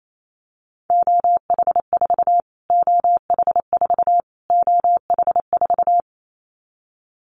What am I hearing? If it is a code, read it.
Morse code, heard as a keyed tone, about 28 wpm, 700 Hz. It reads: OH4 OH4 OH4
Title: Callsign Prefixes for Top 25 DXCC - Repeated 3x - 28wpm